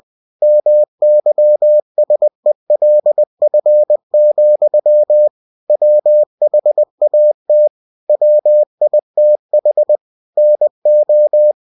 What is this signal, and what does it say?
Morse code, heard as a keyed tone, about 20 wpm, 600 Hz. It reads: MYSELF, WHAT WITH NO
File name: cw-books / winnie-the-pooh / winnie-20wpm-page-0048